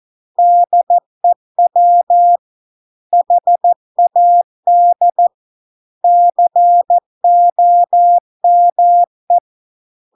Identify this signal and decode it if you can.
Morse code, heard as a keyed tone, about 14 wpm, 700 Hz. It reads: DEW HAD COME